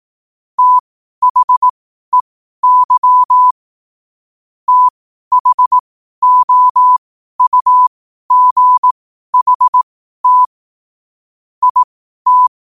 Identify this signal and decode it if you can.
Morse code, heard as a keyed tone, about 18 wpm, spaced as for 12 wpm, 1000 Hz. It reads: THEY THOUGHT IT
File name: qotdfiles / 12wpm